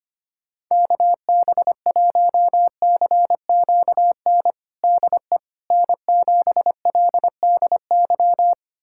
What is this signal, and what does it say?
Morse code, heard as a keyed tone, about 25 wpm, 700 Hz. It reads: KB1CQN DE N7LDY
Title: USA Callsign DE Callsign 25wpm